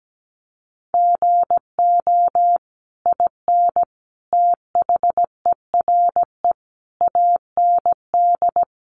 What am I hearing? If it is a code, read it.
Morse code, heard as a keyed tone, about 17 wpm, 700 Hz. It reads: GO IN THERE AND
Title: Sentences from Top 500 Words 17wpm